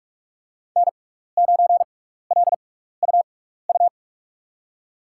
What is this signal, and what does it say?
Morse code, heard as a keyed tone, about 44 wpm, spaced as for 15 wpm, 690 Hz. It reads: N9LUU
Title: USA Call Signs - 5 Characters - 6x Character Spacing - 45wpm